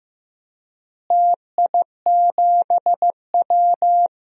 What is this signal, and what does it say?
Morse code, heard as a keyed tone, about 15 wpm, 700 Hz. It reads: TI7W
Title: DXCC-100 Call Signs - 4 Characters - Repeated 3x - 15wpm